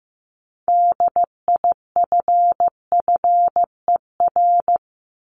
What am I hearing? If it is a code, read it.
Morse code, heard as a keyed tone, about 15 wpm, 700 Hz. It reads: DIFFER
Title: Top 500 Words 15wpm